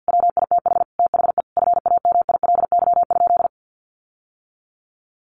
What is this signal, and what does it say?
Morse code, heard as a keyed tone, about 49 wpm, 700 Hz. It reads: WITH THE FAMILY?
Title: Sentences from Top 1100 Words 50wpm